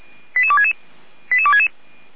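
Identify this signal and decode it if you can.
Five-tone selective call (ZVEI), heard as quick runs of five tones, tones of about 70 ms. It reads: 88266 88266
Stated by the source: ZVEI five-tone call received by SDR, fire brigade Feuerwehr Orlamünde.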